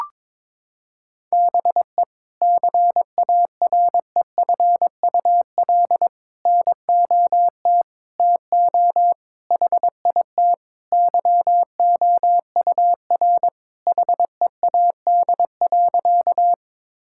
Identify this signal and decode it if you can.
Morse code, heard as a keyed tone, about 22 wpm, 700 Hz. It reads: BE CAREFUL NOT TO HIT YOUR HEAD.
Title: Sentences from Top 1600 Words 22wpm